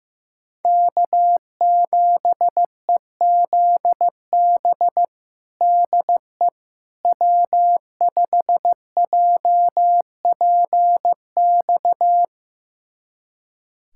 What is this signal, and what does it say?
Morse code, heard as a keyed tone, about 15 wpm, 700 Hz. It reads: K7EZB DE W5JPX
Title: USA Callsign DE Callsign - 5 Characters - 15wpm